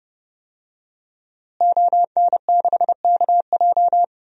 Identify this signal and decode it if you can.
Morse code, heard as a keyed tone, about 32 wpm, 700 Hz. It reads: ON6KJ
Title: DXCC-100 Call Signs - 5 Characters - Repeated 2x - 30wpm